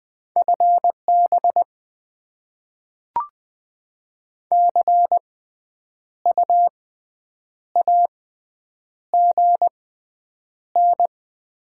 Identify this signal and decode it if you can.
Morse code, heard as a keyed tone, about 20 wpm, spaced as for 8 wpm, 690 Hz. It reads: FB CU AGN
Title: Top 20 QSO Elements - 6x Character Spacing - 20wpm